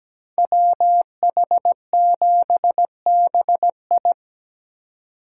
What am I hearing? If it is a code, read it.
Morse code, heard as a keyed tone, about 20 wpm, 700 Hz. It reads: WH7BI
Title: Speed Racing - USA 2x2 Call Signs 17wpm